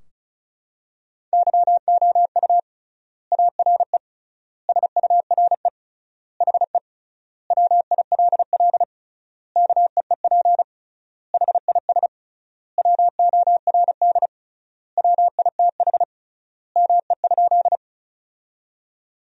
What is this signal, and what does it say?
Morse code, heard as a keyed tone, about 36 wpm, 700 Hz. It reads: YOU ARE SURE HE WILL KEEP HIS WORD WITH ME?
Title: 3x Word-Spacing - Sentences from Top 300 Words - 35wpm